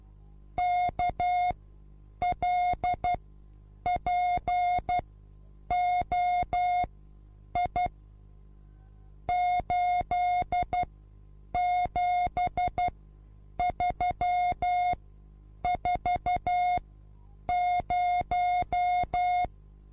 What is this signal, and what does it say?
Morse code, heard as a keyed tone, about 12 wpm, spaced as for 8 wpm, 720 Hz. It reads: KLPOI 87340
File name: repos / Morsekurs / DJ4UF Morsen24